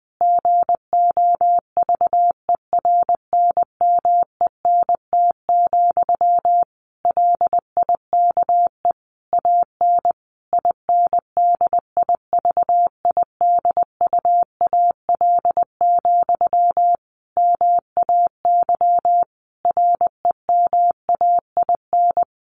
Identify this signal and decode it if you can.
Morse code, heard as a keyed tone, about 20 wpm, 700 Hz. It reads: GOVERNMENT, LIKE AN INDIVIDUAL, MAY REMAIN
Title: Sentences from Top 1400 Words 20wpm